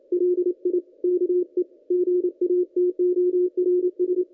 Morse code, heard as a keyed tone, about 27 wpm, 360 Hz, with radio noise. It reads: LIKE GATORS
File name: contests / ss-cw-k5tr-op-n5rz -1800